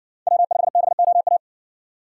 Morse code, heard as a keyed tone, about 58 wpm, 700 Hz. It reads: W5BOA